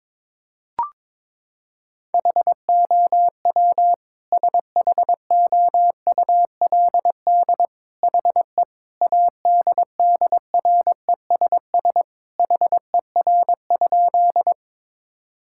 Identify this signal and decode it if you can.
Morse code, heard as a keyed tone, about 22 wpm, 700 Hz. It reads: HOW SHOULD HE ADDRESS HER?
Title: Sentences from Top 1000 Words - Review Part 6 - 22wpm